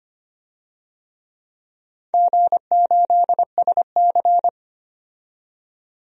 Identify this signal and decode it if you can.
Morse code, heard as a keyed tone, about 25 wpm, 700 Hz. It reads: G8SC
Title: DXCC-100 Call Signs - 4 Characters - 25wpm